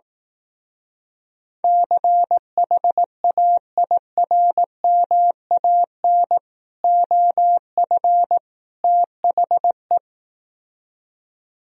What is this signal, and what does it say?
Morse code, heard as a keyed tone, about 18 wpm, 700 Hz. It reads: CHAIRMAN OF THE
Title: Speed-Racing - Sets of 3 Words 15wpm